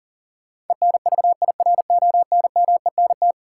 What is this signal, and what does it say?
Morse code, heard as a keyed tone, about 40 wpm, 700 Hz. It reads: ENVIRONMENT
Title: Sentences from Top 1000 Words - Review Part 5 - New Word Drill - 40wpm